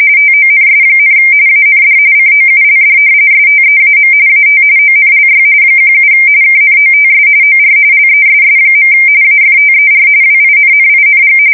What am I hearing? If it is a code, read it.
RTTY, about 45.45 baud. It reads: TONS, AND COULD STEAM AT ONLY 17 KNOTS OR LESS. AFTER WINNING
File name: fsk170 / battleshipnj